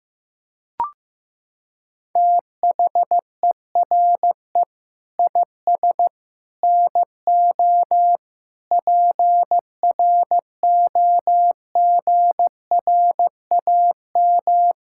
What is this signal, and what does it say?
Morse code, heard as a keyed tone, about 15 wpm, 700 Hz. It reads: THERE IS NO PROGRAM
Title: Sentences from Top 1000 Words - Review Part 5 - 15wpm